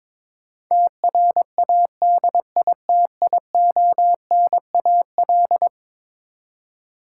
Morse code, heard as a keyed tone, about 22 wpm, 700 Hz. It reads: TRADITIONAL
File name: practice / 22wpm / Eleven Letter Words - 22wpm